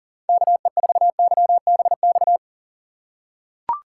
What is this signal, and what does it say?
Morse code, heard as a keyed tone, about 40 wpm, 700 Hz. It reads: KE4YBX
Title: USA Call Signs - 6 Characters - 6x Character Spacing - 40wpm